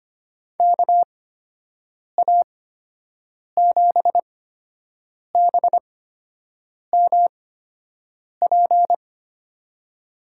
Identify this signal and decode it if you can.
Morse code, heard as a keyed tone, about 25 wpm, spaced as for 7 wpm, 700 Hz. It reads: KA7BMP
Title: USA Call Signs - 6 Characters - 8x Character Spacing - 25wpm